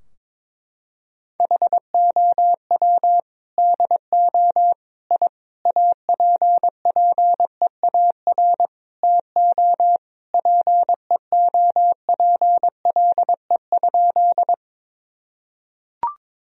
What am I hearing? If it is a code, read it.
Morse code, heard as a keyed tone, about 22 wpm, 700 Hz. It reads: HOW DO I APPEAR TO PEOPLE?
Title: Sentences from Top 1000 Words - Review Part 2 - 22wpm